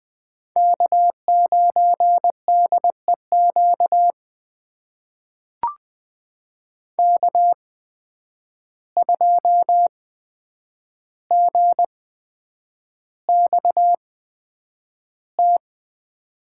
Morse code, heard as a keyed tone, about 20 wpm, spaced as for 9 wpm, 700 Hz. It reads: K9DEQ K2GXT
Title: USA Call Signs - 5 Characters - 8x Character Spacing - 20wpm